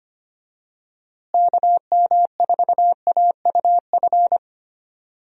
Morse code, heard as a keyed tone, about 25 wpm, 700 Hz. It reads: KM4AUF
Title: USA Call Signs - 6 Characters - Repeated 3x - 25wpm